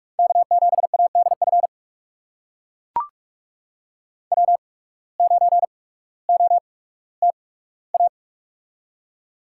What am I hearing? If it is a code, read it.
Morse code, heard as a keyed tone, about 42 wpm, spaced as for 19 wpm, 700 Hz. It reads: K7ADF W9OTA
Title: USA Call Signs - 5 Characters - 8x Character Spacing - 45wpm